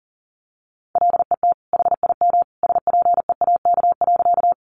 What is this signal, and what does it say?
Morse code, heard as a keyed tone, about 40 wpm, 700 Hz. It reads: LET HIM SPEAK.
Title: Sentences from Top 1100 Words 40wpm